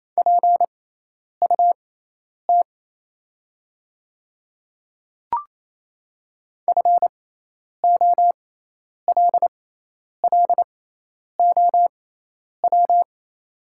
Morse code, heard as a keyed tone, about 27 wpm, spaced as for 10 wpm, 700 Hz. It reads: PUT FOLLOW
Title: 6x Character Spacing - Top 300 Words - 28wpm